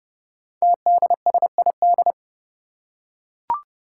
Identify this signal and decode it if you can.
Morse code, heard as a keyed tone, about 29 wpm, 700 Hz. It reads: TDSID
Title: Five Challenging Letter-Number Combinations 30wpm